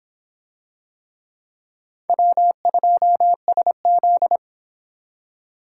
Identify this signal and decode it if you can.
Morse code, heard as a keyed tone, about 24 wpm, 700 Hz. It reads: W2SZ